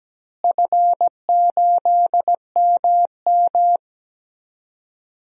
Morse code, heard as a keyed tone, about 17 wpm, 700 Hz. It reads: F8MM
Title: DXCC-100 Call Signs - 4 Characters - Repeated 3x - 17wpm